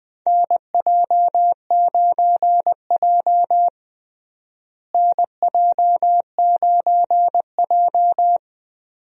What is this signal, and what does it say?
Morse code, heard as a keyed tone, about 20 wpm, 700 Hz. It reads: NJ9J NJ9J
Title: USA Call Signs - 4 Characters - Repeated 2x - 20wpm